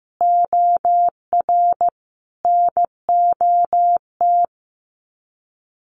Morse code, heard as a keyed tone, about 15 wpm, 700 Hz. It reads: OR NOT
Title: Sets of 5 Words Encore 15wpm